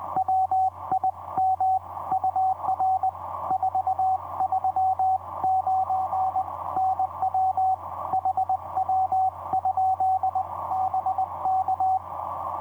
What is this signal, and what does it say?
Morse code, heard as a keyed tone, about 20 wpm, 760 Hz, with radio noise. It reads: WIM UR 439 NW HW? BK
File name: vp8cmh-mm on5jy etc 30m cw 23dec2010 2206